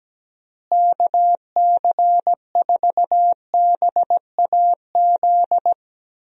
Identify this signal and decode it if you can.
Morse code, heard as a keyed tone, about 17 wpm, 700 Hz. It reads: KC4BAZ
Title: USA Call Signs - 6 Characters - Repeated 3x - 17wpm